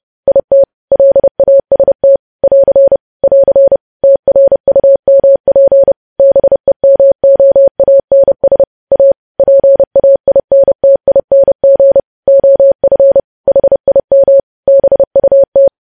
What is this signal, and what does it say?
Morse code, heard as a keyed tone, about 30 wpm, 550 Hz. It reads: IT LAST <AR> <AR> TRUMP BEMOANS A PAINTING OF HIM BUT
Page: news headlines in morse code at 30 wpm